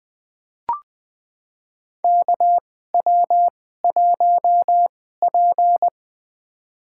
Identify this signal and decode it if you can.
Morse code, heard as a keyed tone, about 20 wpm, spaced as for 15 wpm, 700 Hz. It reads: KW1P
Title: USA Call Signs - 4 Characters - 2x Character Spacing - 20wpm